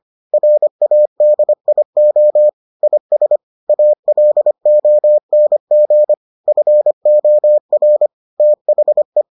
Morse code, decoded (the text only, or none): RADIO IS ALONG FOR THE